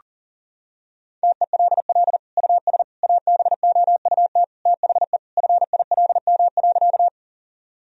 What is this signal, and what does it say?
Morse code, heard as a keyed tone, about 40 wpm, 700 Hz. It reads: TELL US ABOUT THE FILM.
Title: Sentences from Top 1000 Words - Review Part 3 - 40wpm